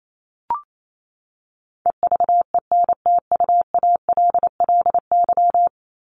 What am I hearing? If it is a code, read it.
Morse code, heard as a keyed tone, about 28 wpm, 700 Hz. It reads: EVENTUALLY
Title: LY Suffix - Top 100 Words with Base Words 28wpm